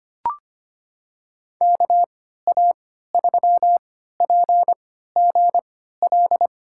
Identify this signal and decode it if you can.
Morse code, heard as a keyed tone, about 25 wpm, spaced as for 14 wpm, 700 Hz. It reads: KA3PGL
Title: USA Call Signs - 6 Characters - 3x Character Spacing - 25wpm